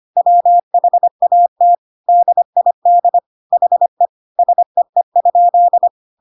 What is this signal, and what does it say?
Morse code, heard as a keyed tone, about 25 wpm, 700 Hz. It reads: WHAT DID HE SEE?
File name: Pinocchio / Pinocchio ch3 25 25 ik1hge